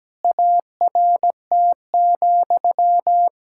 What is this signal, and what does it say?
Morse code, heard as a keyed tone, about 17 wpm, 700 Hz. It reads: ART,
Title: Sentences from Top 1000 Words - Review Part 6 - 17wpm